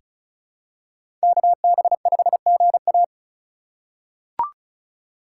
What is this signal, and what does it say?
Morse code, heard as a keyed tone, about 36 wpm, 700 Hz. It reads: KB5GA
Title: K1USN SST QSO Practice - 35wpm